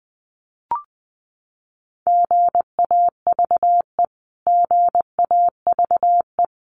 Morse code, heard as a keyed tone, about 20 wpm, 700 Hz. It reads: GAVE GAVE
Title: Top 500 Words Repeated 20wpm